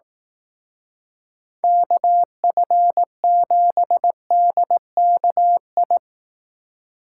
Morse code, heard as a keyed tone, about 18 wpm, 700 Hz. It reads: KF7DKI
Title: Speed Racing - USA 2x3 Call Signs 15wpm